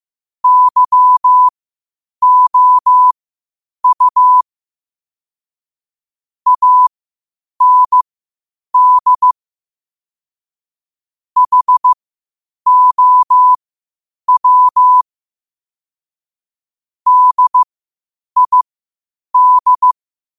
Morse code, heard as a keyed tone, about 15 wpm, spaced as for 8 wpm, 1000 Hz. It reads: YOU AND HOW DID
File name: qotdfiles / 8wpm